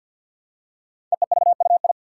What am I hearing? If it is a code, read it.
Morse code, heard as a keyed tone, about 48 wpm, 700 Hz. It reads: EEVUI
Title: Code Groups - 5 Letters - 50wpm